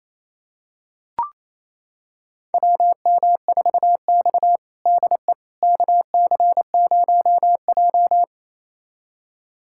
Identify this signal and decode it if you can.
Morse code, heard as a keyed tone, about 28 wpm, 700 Hz. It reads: WM4X DE KC0J
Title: USA Callsign DE Callsign - 4 Characters - 28wpm